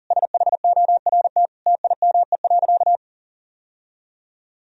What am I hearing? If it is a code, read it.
Morse code, heard as a keyed tone, about 40 wpm, 700 Hz. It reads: SHORT TIME.
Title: Sentences from Top 1000 Words - Review Part 3 - 40wpm